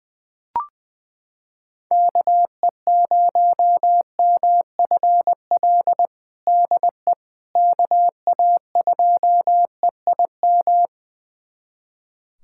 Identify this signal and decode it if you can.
Morse code, heard as a keyed tone, about 20 wpm, 700 Hz. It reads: KE0MFL DE KA2EIM
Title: USA Callsign DE Callsign - 6 Characters - 20wpm